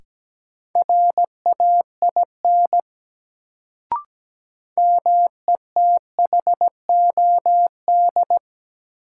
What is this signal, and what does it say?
Morse code, heard as a keyed tone, about 17 wpm, 700 Hz. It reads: RAIN METHOD
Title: Sentences from Top 600 Words - New Word Drill - 17wpm